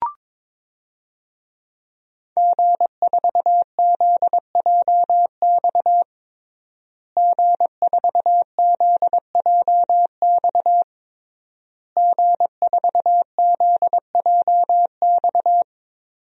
Morse code, heard as a keyed tone, about 22 wpm, 700 Hz. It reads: G4ZJX G4ZJX G4ZJX
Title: DXCC-100 Call Signs - 5 Characters - Repeated 3x - 22wpm